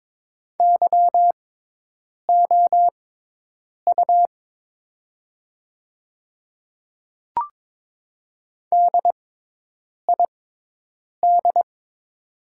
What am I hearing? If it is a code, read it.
Morse code, heard as a keyed tone, about 22 wpm, spaced as for 8 wpm, 700 Hz. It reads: YOU DID
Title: 6x Character Spacing - Top 200 Words - 22wpm